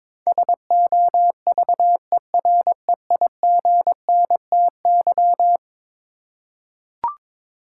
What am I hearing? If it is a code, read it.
Morse code, heard as a keyed tone, about 22 wpm, 700 Hz. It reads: SOVEREIGNTY
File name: practice / 22wpm / Eleven Letter Words - 22wpm